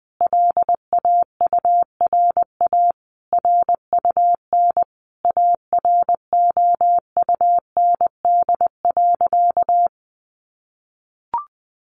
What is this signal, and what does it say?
Morse code, heard as a keyed tone, about 20 wpm, 700 Hz. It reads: LAURA RUN AROUND.
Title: Additional Easy Phrases 20wpm